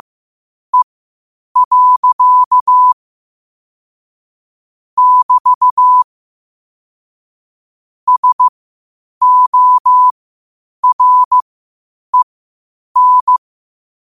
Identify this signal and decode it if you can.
Morse code, heard as a keyed tone, about 15 wpm, spaced as for 8 wpm, 1000 Hz. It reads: E. <BT> SOREN
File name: qotdfiles / 8wpm